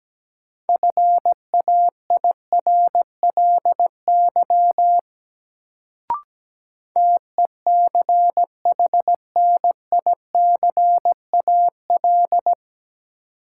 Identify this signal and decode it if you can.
Morse code, heard as a keyed tone, about 17 wpm, 700 Hz. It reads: FAIRLY TECHNICAL
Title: Sentences from Top 1100 Words - New Word Drill - 17wpm